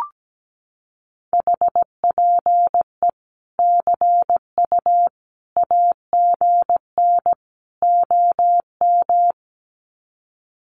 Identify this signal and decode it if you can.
Morse code, heard as a keyed tone, about 17 wpm, 700 Hz. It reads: HPE CU AGN OM
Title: Common QSO Phrases 17wpm